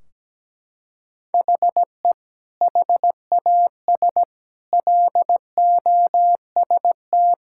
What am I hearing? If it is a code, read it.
Morse code, heard as a keyed tone, about 17 wpm, 700 Hz. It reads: HE HAS LOST
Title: Sentences from Top 1600 Words 17wpm